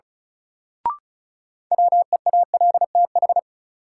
Morse code, heard as a keyed tone, about 36 wpm, 700 Hz. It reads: WEALTH
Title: Sentences from Top 1200 Words - New Word Drill Alt - 35wpm